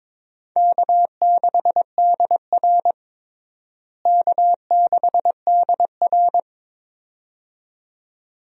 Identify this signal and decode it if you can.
Morse code, heard as a keyed tone, about 21 wpm, 700 Hz. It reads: K6DR K6DR
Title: USA Call Signs - 4 Characters - Repeated 2x - 22wpm